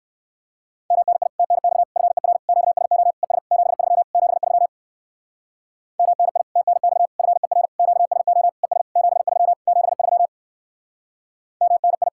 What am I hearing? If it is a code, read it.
Morse code, heard as a keyed tone, about 59 wpm, 700 Hz. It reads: MNI TNX FER QSO ES 73 73 MNI TNX FER QSO ES 73 73 MNI